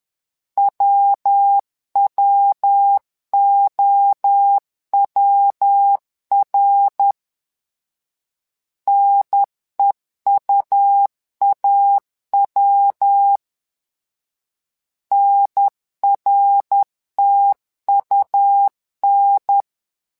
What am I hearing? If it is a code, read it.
Morse code, heard as a keyed tone, about 10 wpm, 800 Hz. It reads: WWOWR NEUAW NRTUN
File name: LEZIONI / 3 MWRO / 3 MWRO 10wpm 2148